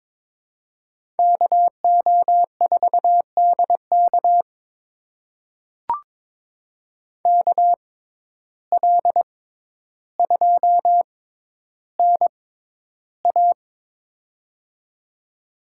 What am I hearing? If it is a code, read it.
Morse code, heard as a keyed tone, about 22 wpm, spaced as for 11 wpm, 700 Hz. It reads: KO4DK KL2NA
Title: USA Call Signs - 5 Characters - 6x Character Spacing - 22wpm